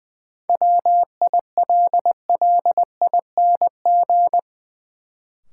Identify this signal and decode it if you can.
Morse code, heard as a keyed tone, about 21 wpm, 700 Hz. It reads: WILLING
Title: Sentences from Top 1000 Words - New Word Drill - 20wpm